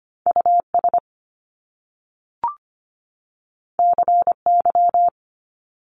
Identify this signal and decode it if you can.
Morse code, heard as a keyed tone, about 25 wpm, 700 Hz. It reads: U S C Y
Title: Top Two Letter Combinations in Words 25wpm